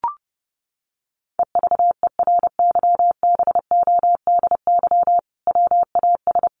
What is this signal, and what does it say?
Morse code, heard as a keyed tone, about 30 wpm, 700 Hz. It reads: EVERYBODY WAS
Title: Sentences from Top 1400 Words 30wpm